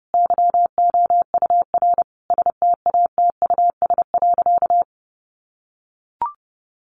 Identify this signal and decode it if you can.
Morse code, heard as a keyed tone, about 30 wpm, 700 Hz. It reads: YOUR STATUS.
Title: Sentences from Top 1400 Words 30wpm